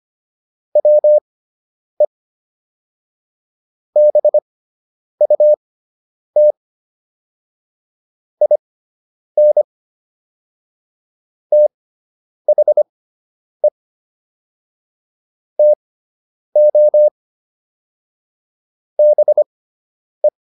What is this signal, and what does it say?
Morse code, heard as a keyed tone, about 25 wpm, spaced as for 9 wpm, 600 Hz. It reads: WE BUT IN THE TO BE